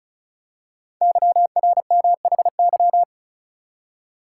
Morse code, heard as a keyed tone, about 36 wpm, 690 Hz. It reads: YRMHY
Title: Code Groups - 5 Letters - 35wpm